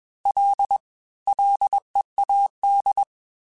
Morse code, heard as a keyed tone, about 21 wpm, 800 Hz. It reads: L LEAD
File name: F and L / F and L mp3 / F and L Prefixes 3x2 Twice @20